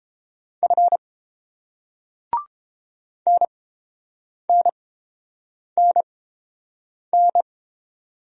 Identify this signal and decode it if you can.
Morse code, heard as a keyed tone, about 31 wpm, spaced as for 7 wpm, 700 Hz. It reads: F NNNN